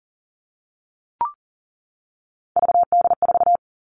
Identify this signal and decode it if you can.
Morse code, heard as a keyed tone, about 40 wpm, 700 Hz. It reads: VD4
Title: Three Challenging Letter-Number Combinations 40wpm